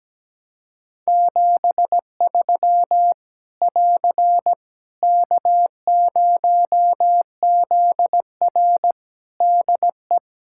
Morse code, heard as a keyed tone, about 17 wpm, 700 Hz. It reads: 73 <AR> K0ZR DE
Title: Practice QSO - Exchange 4 - Ending - 17wpm